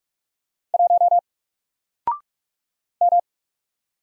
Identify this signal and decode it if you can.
Morse code, heard as a keyed tone, about 44 wpm, 700 Hz. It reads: 1 M
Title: Single Letter-Number Alt 45wpm